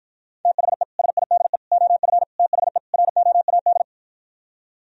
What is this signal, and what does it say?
Morse code, heard as a keyed tone, about 53 wpm, 700 Hz. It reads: THE SIDE OF THE ROAD